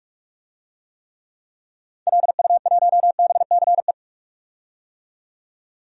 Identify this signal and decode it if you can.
Morse code, heard as a keyed tone, about 46 wpm, 700 Hz. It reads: LU1BCE
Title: DXCC-100 Call Signs - 6 Characters - 45wpm